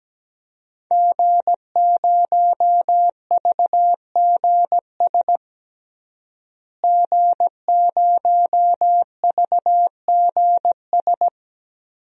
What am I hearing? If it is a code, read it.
Morse code, heard as a keyed tone, about 17 wpm, 700 Hz. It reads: G0VGS G0VGS
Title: DXCC-100 Call Signs - 5 Characters - Repeated 3x - 17wpm